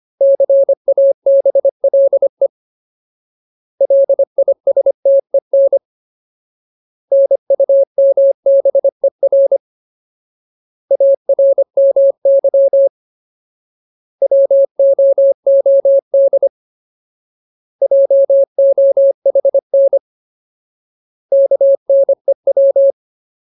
Morse code, heard as a keyed tone, about 25 wpm, 550 Hz. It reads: CABLE LISTEN NUMBER ARMY WOOD JOHN KNEW
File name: Ultimate Word List / 25WPM Ultimate